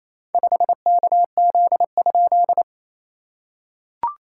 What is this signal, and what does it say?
Morse code, heard as a keyed tone, about 28 wpm, 690 Hz. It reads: 5KZ?
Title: SOTA QSO Practice 28wpm